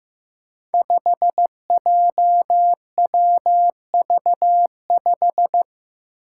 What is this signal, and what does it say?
Morse code, heard as a keyed tone, about 15 wpm, 700 Hz. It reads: 5JWV5
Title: Five Challenging Letter-Number Combinations 15wpm